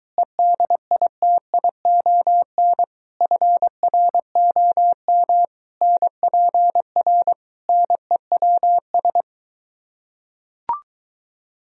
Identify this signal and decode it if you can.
Morse code, heard as a keyed tone, about 22 wpm, 700 Hz. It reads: EDITION FROM NPR NEWS